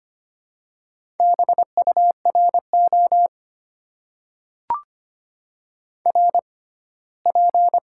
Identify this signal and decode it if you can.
Morse code, heard as a keyed tone, about 25 wpm, 700 Hz. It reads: B U R O R P
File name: Top 90 QSO Elements - 6x Character Spacing - 25wpm